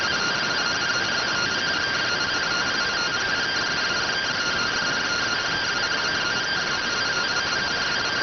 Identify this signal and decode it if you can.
Two-tone teleprinter signal, about 170 Hz shift.